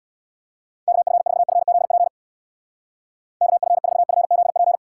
Morse code, heard as a keyed tone, about 65 wpm, 700 Hz. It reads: KF4FZP KF4FZP